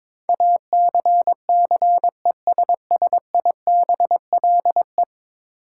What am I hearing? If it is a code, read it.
Morse code, heard as a keyed tone, about 22 wpm, 700 Hz. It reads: ACCESSIBLE
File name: practice / 22wpm / 10 Letter Words - First 3 Letters Given - 22wpm